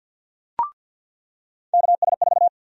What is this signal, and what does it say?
Morse code, heard as a keyed tone, about 53 wpm, 700 Hz. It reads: KS4
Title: USA Call Sign Prefixes - 50wpm